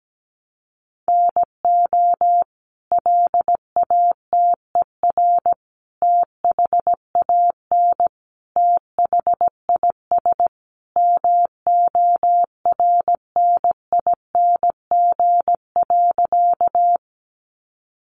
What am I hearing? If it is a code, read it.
Morse code, heard as a keyed tone, about 17 wpm, 700 Hz. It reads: NO LATER THAN THIS MORNING.
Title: Sentences from Top 700 Words 17wpm